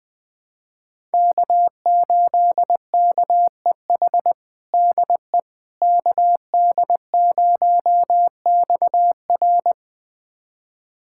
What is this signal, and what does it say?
Morse code, heard as a keyed tone, about 20 wpm, 700 Hz. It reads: K8KEH DE KD0XR
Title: USA Callsign DE Callsign - 5 Characters - 20wpm